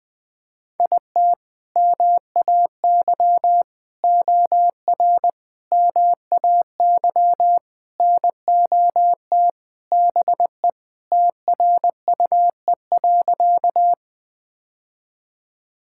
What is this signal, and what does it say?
Morse code, heard as a keyed tone, about 20 wpm, 700 Hz. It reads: IT MAY OR MAY NOT BE TRUE.
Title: Sentences with 5-Word Phrases - 20wpm